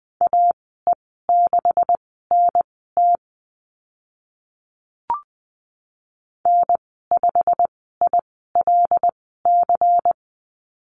Morse code, heard as a keyed tone, about 20 wpm, spaced as for 15 wpm, 700 Hz. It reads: AE6NT N5ILC
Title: 140 Lesson 23g - Letter f - Call Signs 20wpm